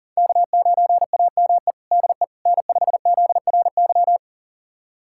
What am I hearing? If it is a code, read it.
Morse code, heard as a keyed tone, about 40 wpm, 700 Hz. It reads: K9AME DE N5ZRY
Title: USA Callsign DE Callsign - 5 Characters - 40wpm